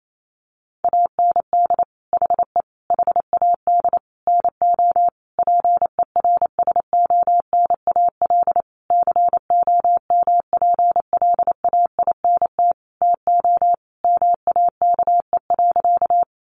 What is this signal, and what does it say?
Morse code, heard as a keyed tone, about 28 wpm, 700 Hz. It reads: AND HE HAD NO PERSONAL COMPLAINT TO MAKE.
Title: Sentences from Top 1400 Words 28wpm